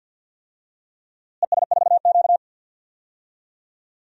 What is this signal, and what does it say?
Morse code, heard as a keyed tone, about 52 wpm, 700 Hz. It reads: ES4Q
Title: DXCC-100 Call Signs - 4 Characters - Repeated 2x - 50wpm